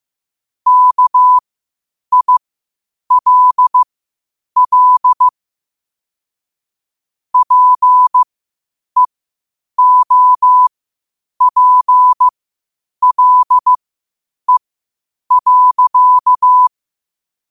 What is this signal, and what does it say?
Morse code, heard as a keyed tone, about 15 wpm, spaced as for 8 wpm, 1000 Hz. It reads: KILL PEOPLE.
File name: qotdfiles / 8wpm